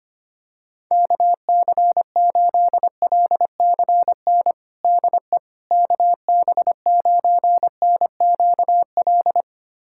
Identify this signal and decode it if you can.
Morse code, heard as a keyed tone, about 25 wpm, 700 Hz. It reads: KC8LCN DE KB9NQL
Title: USA Callsign DE Callsign - 6 Characters - 25wpm